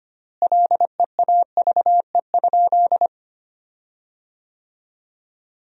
Morse code, heard as a keyed tone, about 25 wpm, 700 Hz. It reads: LEAVE?
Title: 4x Word-Spacing - Sentences from Top 400 Words - 25wpm